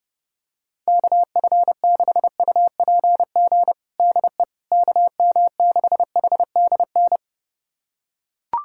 Morse code, heard as a keyed tone, about 30 wpm, 700 Hz. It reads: KF6UPG DE KM6HDN
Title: USA Callsign DE Callsign - 6 Characters - 30wpm